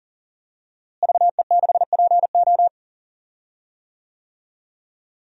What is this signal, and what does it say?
Morse code, heard as a keyed tone, about 40 wpm, 700 Hz. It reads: VE6PO
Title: DXCC-100 Call Signs - 5 Characters - Repeated 3x - 40wpm